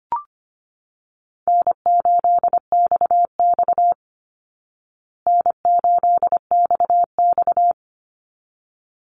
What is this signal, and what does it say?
Morse code, heard as a keyed tone, about 25 wpm, 700 Hz. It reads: N8XX N8XX
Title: Easy Call Signs Repeated 2x - 25wpm